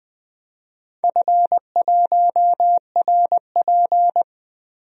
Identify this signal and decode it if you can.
Morse code, heard as a keyed tone, about 20 wpm, 700 Hz. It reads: F1RP
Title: DXCC-100 Call Signs - 4 Characters - 20wpm